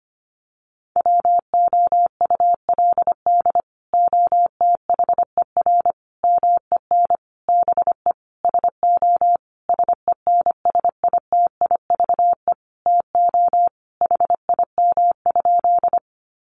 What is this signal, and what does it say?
Morse code, heard as a keyed tone, about 25 wpm, 700 Hz. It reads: WOULD OTHER MEN BE SO SENSITIVE TO HIM?
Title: Sentences from Top 1400 Words 25wpm